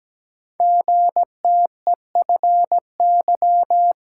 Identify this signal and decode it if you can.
Morse code, heard as a keyed tone, about 17 wpm, 700 Hz. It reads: GTEFY
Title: Code Groups - 5 Letter:Numbers - 17wpm